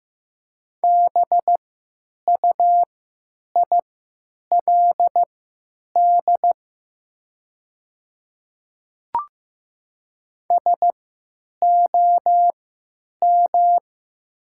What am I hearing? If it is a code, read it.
Morse code, heard as a keyed tone, about 15 wpm, spaced as for 9 wpm, 700 Hz. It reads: BUILD SOM
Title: 3x Character Spacing - Top 200 Words - 15wpm